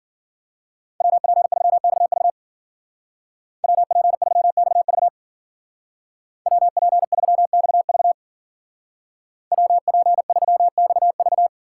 Morse code, heard as a keyed tone, about 53 wpm, 700 Hz. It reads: WP3XV WP3XV WP3XV WP3XV